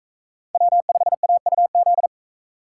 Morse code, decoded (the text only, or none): W5AUZ